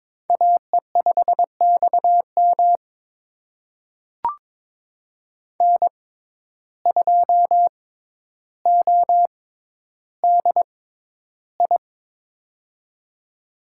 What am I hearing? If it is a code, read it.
Morse code, heard as a keyed tone, about 22 wpm, spaced as for 11 wpm, 700 Hz. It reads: AE5XM N2ODI
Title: USA Call Signs - 5 Characters - 6x Character Spacing - 22wpm